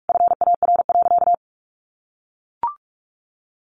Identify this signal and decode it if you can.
Morse code, heard as a keyed tone, about 44 wpm, 700 Hz. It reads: FAR.
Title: Sentences from Top 1500 Words 45wpm